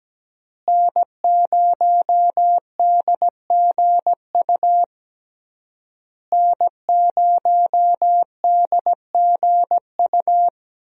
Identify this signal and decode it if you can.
Morse code, heard as a keyed tone, about 17 wpm, 700 Hz. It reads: N0DGU N0DGU
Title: USA Call Signs - 5 Characters - Repeated 3x - 17wpm